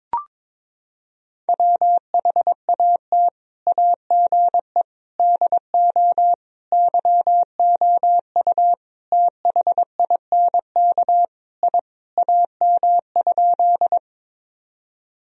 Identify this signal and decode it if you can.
Morse code, heard as a keyed tone, about 22 wpm, 700 Hz. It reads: WHAT AGE DO YOU THINK I AM?
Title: Sentences from Top 1000 Words - Review Part 3 - 22wpm